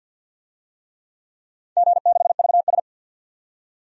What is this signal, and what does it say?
Morse code, heard as a keyed tone, about 51 wpm, 700 Hz. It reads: G6VS
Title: DXCC-100 Call Signs - 4 Characters - 50wpm